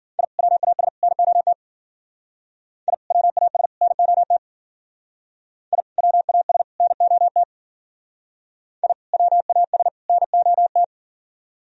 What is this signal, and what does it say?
Morse code, heard as a keyed tone, about 51 wpm, 700 Hz. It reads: I WAS NOT I WAS NOT I WAS NOT I WAS NOT